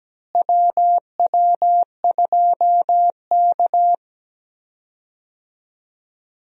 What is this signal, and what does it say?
Morse code, heard as a keyed tone, about 17 wpm, 700 Hz. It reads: WW2K
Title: USA Call Signs - 4 Characters - Repeated 3x - 17wpm